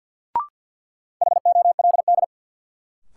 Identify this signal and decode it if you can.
Morse code, heard as a keyed tone, about 53 wpm, 700 Hz. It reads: HOLD
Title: Sentences from Top 400 Words - New Word Drill - 50wpm